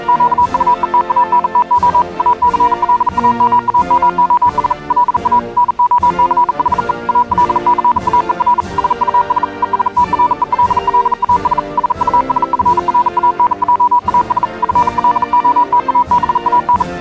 Morse code, heard as a keyed tone, about 37 wpm, 980 Hz. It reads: QUE TANTO ACLAMAMOS LA NOCHE AL CAER SUS ESTRELLAS SUS FRANJAS FLOTABAN